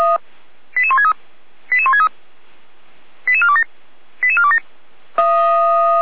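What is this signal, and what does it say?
Five-tone selective call (ZVEI), heard as quick runs of five tones, tones of about 70 ms. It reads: 88163 88163 88427 88427
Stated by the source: ZVEI five-tone selective call, address 88205, received on an SDR and logged by Feuerwehr Orlamünde (f85360).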